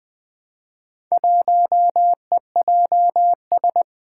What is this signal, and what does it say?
Morse code, heard as a keyed tone, about 20 wpm, 700 Hz. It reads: 1EJS
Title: Four Challenging Letter-Number Combinations 20wpm